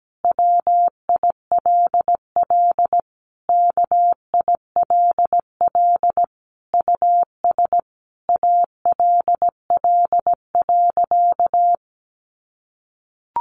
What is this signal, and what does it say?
Morse code, heard as a keyed tone, about 17 wpm, 700 Hz. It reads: WILL KILL US ALL.
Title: Sentences from Top 1400 Words 17wpm